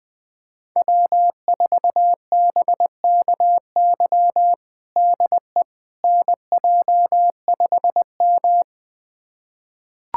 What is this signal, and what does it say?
Morse code, heard as a keyed tone, about 21 wpm, 700 Hz. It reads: W4BKY DE NJ5M
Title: USA Callsign DE Callsign 20wpm